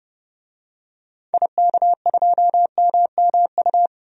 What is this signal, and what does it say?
Morse code, heard as a keyed tone, about 30 wpm, 700 Hz. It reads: IK2MMU
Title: DXCC-100 Call Signs - 6 Characters - 30wpm